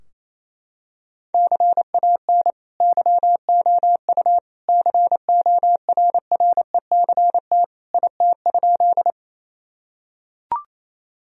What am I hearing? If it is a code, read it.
Morse code, heard as a keyed tone, about 28 wpm, 700 Hz. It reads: CAN YOU CORRECT IT?
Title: Sentences from Top 1000 Words - Review Part 2 - 28wpm